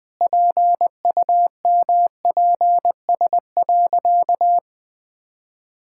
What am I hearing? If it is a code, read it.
Morse code, heard as a keyed tone, about 20 wpm, 700 Hz. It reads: PUMPS.
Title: Pangram Challenge v2 - 20wpm